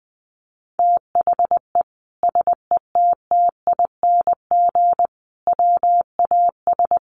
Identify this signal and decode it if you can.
Morse code, heard as a keyed tone, about 20 wpm, 700 Hz. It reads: THE SETTING WAS
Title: Sentences from Top 1300 Words 20wpm